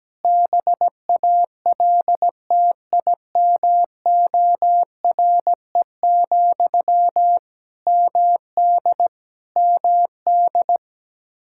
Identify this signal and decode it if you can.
Morse code, heard as a keyed tone, about 17 wpm, 700 Hz. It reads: BALTIMORE, MD MD
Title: Practice QSO - Exchange 1 - Beginning - 17wpm